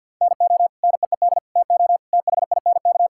Morse code, heard as a keyed tone, about 50 wpm, 700 Hz. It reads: NO NEED TO THINK
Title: Sentences from Top 1600 Words 50wpm